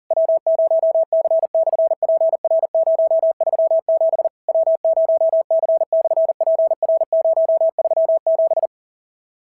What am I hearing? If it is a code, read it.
Morse code, heard as a keyed tone, about 42 wpm, 650 Hz. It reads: W0C/PR037 W0C/PR037
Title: SOTA QSO Practice 40wpm